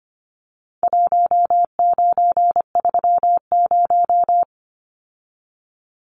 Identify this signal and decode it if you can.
Morse code, heard as a keyed tone, about 25 wpm, 700 Hz. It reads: 1930
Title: Numbers - Contest Style - 4-Digit 25wpm